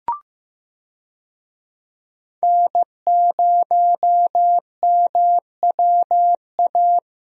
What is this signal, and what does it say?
Morse code, heard as a keyed tone, about 15 wpm, 700 Hz. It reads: N0MWA
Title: USA Call Signs - 5 Characters - Repeated 2x - 15wpm